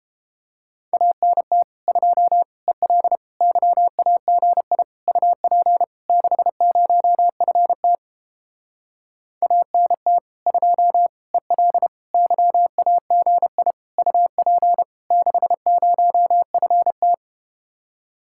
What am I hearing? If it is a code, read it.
Morse code, heard as a keyed tone, about 31 wpm, 700 Hz. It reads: ANT 2 EL YAGI UP 60FT ANT 2 EL YAGI UP 60FT